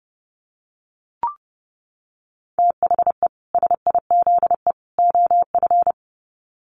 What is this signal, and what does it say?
Morse code, heard as a keyed tone, about 30 wpm, 700 Hz. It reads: THE SIZE OF
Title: Sets of 3 Words 30wpm